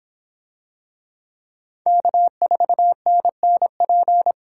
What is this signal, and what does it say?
Morse code, heard as a keyed tone, about 24 wpm, 700 Hz. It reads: K4NNP